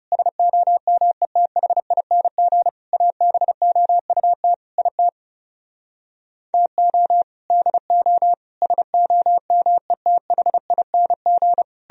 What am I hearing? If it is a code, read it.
Morse code, heard as a keyed tone, about 39 wpm, 700 Hz. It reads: SOMETHING ABOUT IT TO DO SOMETHING